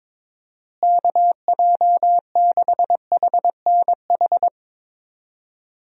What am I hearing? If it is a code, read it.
Morse code, heard as a keyed tone, about 22 wpm, 700 Hz. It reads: KJ6HNH
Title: USA Call Signs - 6 Characters - Repeated 3x - 22wpm